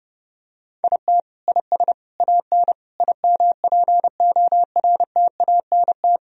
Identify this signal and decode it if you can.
Morse code, heard as a keyed tone, about 30 wpm, 700 Hz. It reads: IT IS AN IMPORTANT
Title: Sentences from Top 1000 Words - Review Part 1 - 30wpm